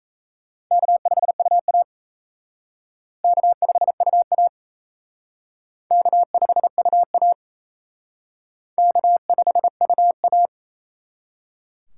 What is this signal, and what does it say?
Morse code, heard as a keyed tone, about 34 wpm, 700 Hz. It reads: K5UA K5UA K5UA K5UA
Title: Speed Racing - USA 1x2 Call Signs 28wpm